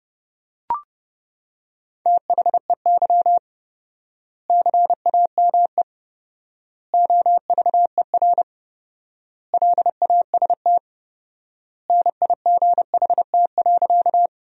Morse code, heard as a keyed tone, about 30 wpm, 700 Hz. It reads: THEY CAME OVER LAST NIGHT.
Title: 4x Word-Spacing - Sentences from Top 300 Words - 30wpm